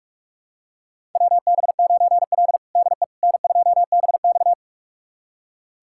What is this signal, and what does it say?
Morse code, heard as a keyed tone, about 46 wpm, 700 Hz. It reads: WB9L DE N2BX
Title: USA Callsign DE Callsign - 4 Characters - 45wpm